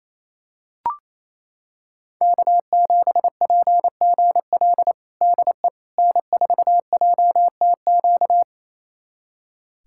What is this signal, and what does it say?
Morse code, heard as a keyed tone, about 28 wpm, 700 Hz. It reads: K7PGL DE N4JTQ
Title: USA Callsign DE Callsign - 5 Characters - 28wpm